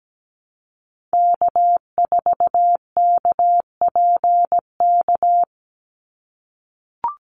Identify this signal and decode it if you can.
Morse code, heard as a keyed tone, about 17 wpm, 700 Hz. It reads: K4KPK
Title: NA SOTA Activator Callsigns 17wpm